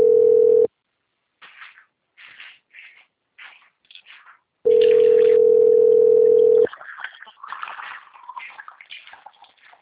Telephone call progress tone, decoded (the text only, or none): ringback tone